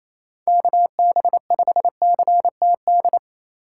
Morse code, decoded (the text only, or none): KB5CTD